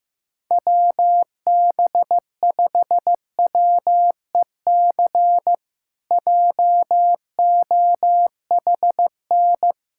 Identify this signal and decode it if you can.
Morse code, heard as a keyed tone, about 16 wpm, 700 Hz. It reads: WB5WEC JOHN
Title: K1USN SST QSO Practice - 15wpm